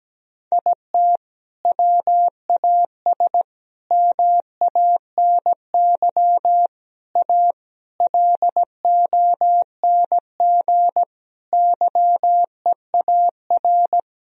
Morse code, heard as a keyed tone, about 17 wpm, 700 Hz. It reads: IT WAS MANY A LONG YEAR
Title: Sentences from Top 1000 Words - Review Part 1 - 17wpm